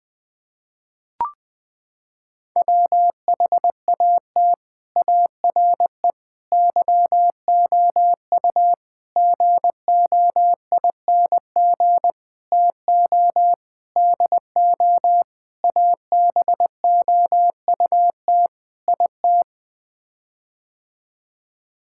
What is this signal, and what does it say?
Morse code, heard as a keyed tone, about 20 wpm, 700 Hz. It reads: WHAT ARE YOU GOING TO DO ABOUT IT
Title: Sets of 8 Words - 20wpm